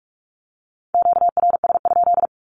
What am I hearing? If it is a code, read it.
Morse code, heard as a keyed tone, about 44 wpm, 700 Hz. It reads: QRS?
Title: Q-Codes 45wpm